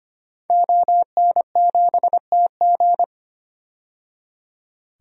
Morse code, heard as a keyed tone, about 25 wpm, 700 Hz. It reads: ON7TG
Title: DXCC-100 Call Signs - 5 Characters - 25wpm